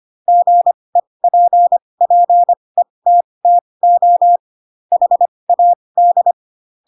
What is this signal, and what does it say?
Morse code, heard as a keyed tone, about 25 wpm, spaced as for 20 wpm, 700 Hz. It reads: GEPPETTO HAD
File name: Pinocchio / Pinocchio ch2 15 25 ik1hge